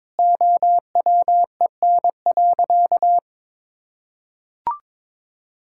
Morse code, heard as a keyed tone, about 21 wpm, 700 Hz. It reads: OWEN.
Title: Pangram Challenge v2 - 22wpm